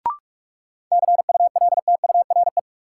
Morse code, heard as a keyed tone, about 46 wpm, 700 Hz. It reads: CULTURE
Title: Sentences from Top 1000 Words - Review Part 6 - New Word Drill - 45wpm